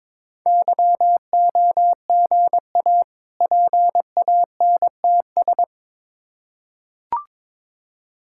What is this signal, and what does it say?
Morse code, heard as a keyed tone, about 22 wpm, 690 Hz. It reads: YOGA PANTS
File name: Difficult Words with Hints - 22wpm